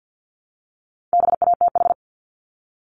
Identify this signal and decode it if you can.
Morse code, heard as a keyed tone, about 50 wpm, 690 Hz. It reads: BATH
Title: Multi-letter Phonemes Repeated 3x 50wpm